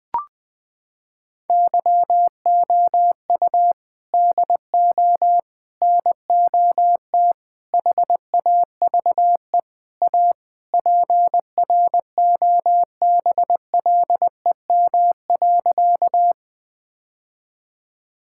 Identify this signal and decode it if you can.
Morse code, heard as a keyed tone, about 20 wpm, 700 Hz. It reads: YOU DO NOT HAVE A PROBLEM.
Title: Sentences with 5-Word Phrases - 20wpm